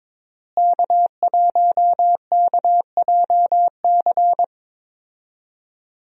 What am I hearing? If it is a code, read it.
Morse code, heard as a keyed tone, about 21 wpm, 700 Hz. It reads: K1KJC
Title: USA Call Signs - 5 Characters - Repeated 3x - 22wpm